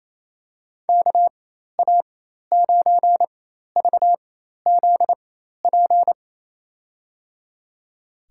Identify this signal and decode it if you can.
Morse code, heard as a keyed tone, about 27 wpm, spaced as for 13 wpm, 700 Hz. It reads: KA9VZP
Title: USA Call Signs - 6 Characters - 4x Character Spacing - 28wpm